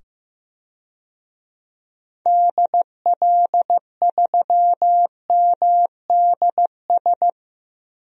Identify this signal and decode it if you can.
Morse code, heard as a keyed tone, about 16 wpm, 690 Hz. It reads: DL3MDS
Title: DXCC-100 Call Signs - 6 Characters - 15wpm